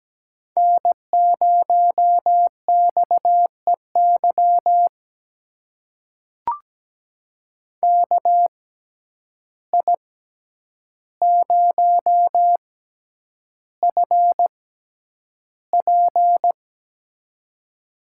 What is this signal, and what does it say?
Morse code, heard as a keyed tone, about 17 wpm, spaced as for 9 wpm, 700 Hz. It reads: N0XEY KI0FP
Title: USA Call Signs - 5 Characters - 6x Character Spacing - 17wpm